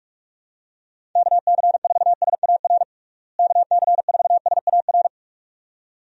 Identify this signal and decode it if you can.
Morse code, heard as a keyed tone, about 46 wpm, 700 Hz. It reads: KC4SAR KC4SAR
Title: USA Call Signs - 6 Characters - Repeated 2x - 45wpm